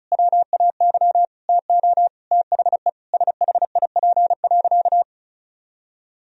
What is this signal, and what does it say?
Morse code, heard as a keyed tone, about 36 wpm, 700 Hz. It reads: WAY TO THE SHIP.
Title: Sentences from Top 1000 Words - Review Part 2 - 35wpm